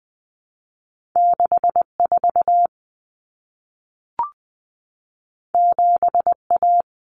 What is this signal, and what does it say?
Morse code, heard as a keyed tone, about 20 wpm, 700 Hz. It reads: 64 7A
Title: Common Cut Numbers - 2-Digit 20wpm